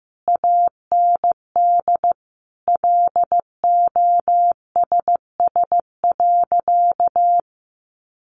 Morse code, heard as a keyed tone, about 15 wpm, 700 Hz. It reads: AND LOSS.
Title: Sentences from Top 1400 Words 15wpm